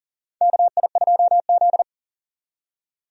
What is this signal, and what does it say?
Morse code, heard as a keyed tone, about 40 wpm, 700 Hz. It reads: KI2Z
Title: USA Call Signs - 4 Characters - 40wpm